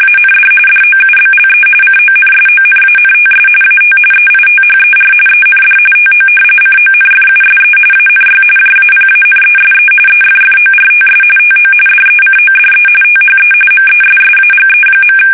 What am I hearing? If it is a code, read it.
RTTY, about 45.45 baud. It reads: AND DOUBLE CONSONANTS AT EITHER END TO CONFUSE ANY JAPANESE EFFORTS TO DECODE THE MESSAGE.
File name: fsk850 / battleshipnj